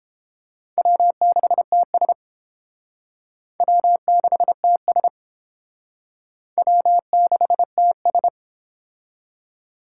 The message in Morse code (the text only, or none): W6TS W6TS W6TS